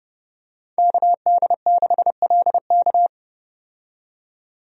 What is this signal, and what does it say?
Morse code, heard as a keyed tone, about 30 wpm, 700 Hz. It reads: KD6LK
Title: USA Call Signs - 5 Characters - 30wpm